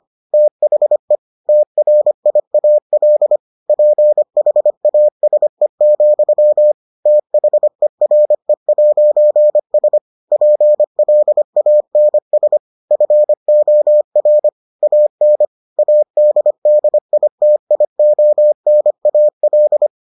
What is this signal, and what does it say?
Morse code, heard as a keyed tone, about 25 wpm, 600 Hz. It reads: THE TRIAL PHASE, THERE'S PLANS FOR AN ADDITIONAL